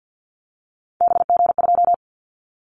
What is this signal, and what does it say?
Morse code, heard as a keyed tone, about 50 wpm, 690 Hz. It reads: BG2
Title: Callsign Prefixes for Top 100 DXCC - 50wpm